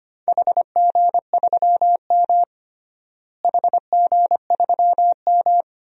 Morse code, heard as a keyed tone, about 25 wpm, 700 Hz. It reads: HG3M HG3M
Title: DXCC-100 Call Signs - 4 Characters - Repeated 2x - 25wpm